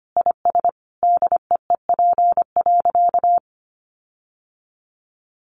Morse code, heard as a keyed tone, about 25 wpm, 700 Hz. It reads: IS DEEP.
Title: Sentences from Top 1500 Words 25wpm